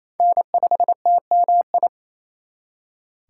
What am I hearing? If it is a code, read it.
Morse code, heard as a keyed tone, about 34 wpm, 700 Hz. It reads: N5TMI
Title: Speed Racing - USA Call Signs 28wpm